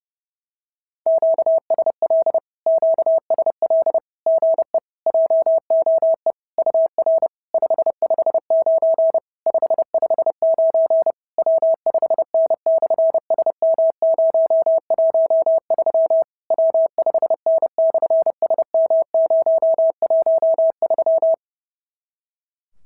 Morse code, heard as a keyed tone, about 32 wpm, 650 Hz. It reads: QSL QSL GE JOE UR 559 559 W5N/SM013 W5N/SM013
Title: SOTA QSO Practice 30wpm